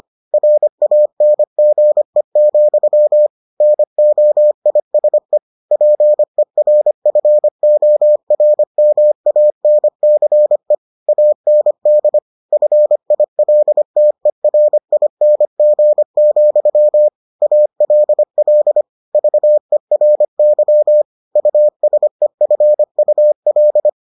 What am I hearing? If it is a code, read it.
Morse code, heard as a keyed tone, about 25 wpm, 600 Hz. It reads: RANGE, NOISE PERFORMANCE AND FILTERING, ALL VERY USEFUL